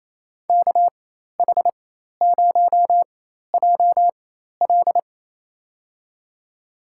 Morse code, heard as a keyed tone, about 28 wpm, spaced as for 13 wpm, 700 Hz. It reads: KH0JL
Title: USA Call Signs - 4x Character Spacing - 28wpm